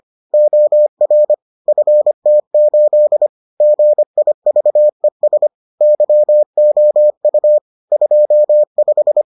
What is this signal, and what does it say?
Morse code, heard as a keyed tone, about 25 wpm, 600 Hz. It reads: OR FT8 GIVES YOU 25